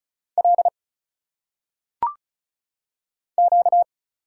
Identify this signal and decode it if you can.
Morse code, heard as a keyed tone, about 36 wpm, 700 Hz. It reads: L Q
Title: Single Letters - Phonetic Pronunciation 35wpm